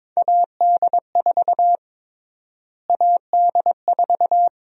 Morse code, heard as a keyed tone, about 22 wpm, 700 Hz. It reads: AD4 AD4
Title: USA Call Sign Prefixes - Repeated 2x - 22wpm